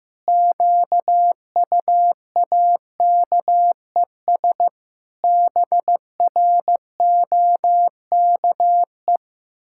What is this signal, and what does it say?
Morse code, heard as a keyed tone, about 15 wpm, 700 Hz. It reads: QUAKES BROKE
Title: Pangram Challenge v2 - 15wpm